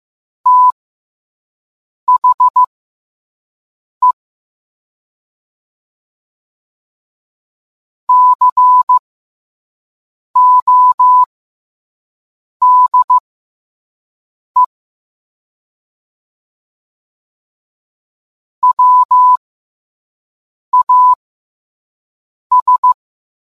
Morse code, heard as a keyed tone, about 15 wpm, spaced as for 5 wpm, 1000 Hz. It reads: THE CODE WAS